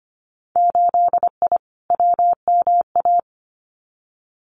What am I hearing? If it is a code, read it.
Morse code, heard as a keyed tone, about 25 wpm, 700 Hz. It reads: 8I WMA
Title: Winter Field Day Exchanges Encore 25wpm